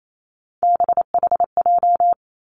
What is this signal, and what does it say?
Morse code, heard as a keyed tone, about 28 wpm, 700 Hz. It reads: BHJ
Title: Three Challenging Letter-Number Combinations 28wpm